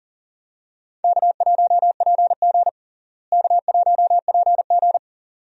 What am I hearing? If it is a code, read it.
Morse code, heard as a keyed tone, about 44 wpm, 700 Hz. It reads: K1PG K1PG
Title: USA Call Signs - 4 Characters - Repeated 3x - 40wpm